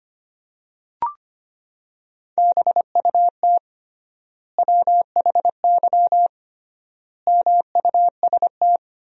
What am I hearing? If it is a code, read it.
Morse code, heard as a keyed tone, about 25 wpm, 700 Hz. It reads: BUT WHY MUST
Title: 3x Word-Spacing - Sentences from Top 400 Words - 25wpm